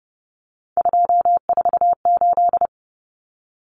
Morse code, heard as a keyed tone, about 30 wpm, 700 Hz. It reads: 248
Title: Numbers - 3-Digit 30wpm